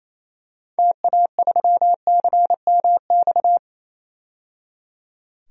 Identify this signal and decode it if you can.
Morse code, heard as a keyed tone, about 28 wpm, 700 Hz. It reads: TA3CMX
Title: DXCC-100 Call Signs - 6 Characters - 28wpm